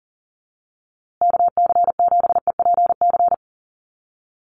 Morse code, heard as a keyed tone, about 40 wpm, 700 Hz. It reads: KC7EPC
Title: USA Call Signs Repeated 3x 40wpm